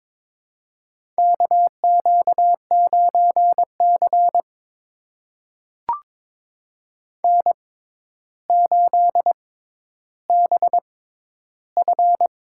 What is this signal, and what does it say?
Morse code, heard as a keyed tone, about 22 wpm, spaced as for 11 wpm, 700 Hz. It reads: KQ9C N8BF
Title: USA Call Signs - 4 Characters - 6x Character Spacing - 22wpm